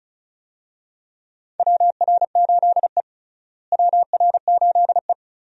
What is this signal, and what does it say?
Morse code, heard as a keyed tone, about 36 wpm, 700 Hz. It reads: WR8E WR8E
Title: USA Call Signs - 4 Characters - Repeated 3x - 35wpm